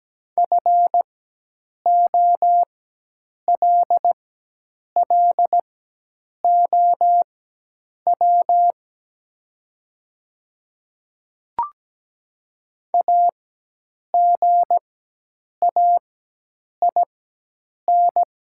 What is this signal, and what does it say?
Morse code, heard as a keyed tone, about 17 wpm, spaced as for 8 wpm, 700 Hz. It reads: FOLLOW AGAIN
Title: 4x Character Spacing - Top 200 Words - 17wpm